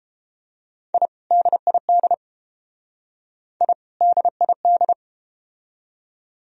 Morse code, heard as a keyed tone, about 31 wpm, 700 Hz. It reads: I DID I DID